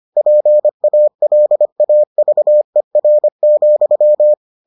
Morse code, heard as a keyed tone, about 25 wpm, 600 Hz. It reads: PALAVER,